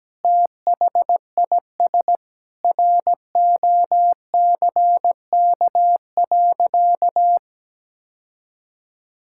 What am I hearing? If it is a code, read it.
Morse code, heard as a keyed tone, about 17 wpm, 700 Hz. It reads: THIS ROCK.
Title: Sentences from Top 1600 Words 17wpm